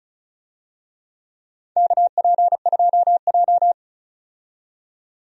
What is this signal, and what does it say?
Morse code, heard as a keyed tone, about 36 wpm, 700 Hz. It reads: KP2J
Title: DXCC-100 Call Signs - 4 Characters - 35wpm